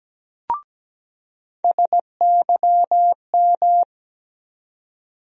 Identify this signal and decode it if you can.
Morse code, heard as a keyed tone, about 17 wpm, 700 Hz. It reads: SYM
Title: USA Call Sign Postfixes - 3 Letters 17wpm